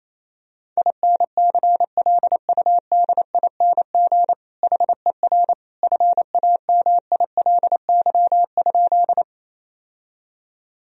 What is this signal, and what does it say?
Morse code, heard as a keyed tone, about 28 wpm, 700 Hz. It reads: INCLUDING HER FAMILY?
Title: Sentences from Top 1000 Words - Review Part 4 - 28wpm